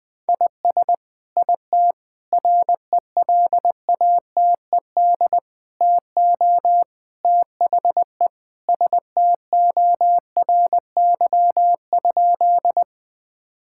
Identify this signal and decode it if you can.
Morse code, heard as a keyed tone, about 20 wpm, 700 Hz. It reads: IS IT RELATED TO THE STORY?
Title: Sentences from Top 1000 Words - Review Part 5 - 20wpm